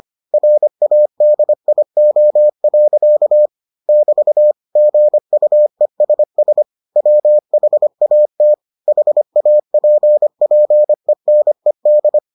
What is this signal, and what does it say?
Morse code, heard as a keyed tone, about 25 wpm, 600 Hz. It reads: RADIO. <BT> GUESS WHAT HAPPENED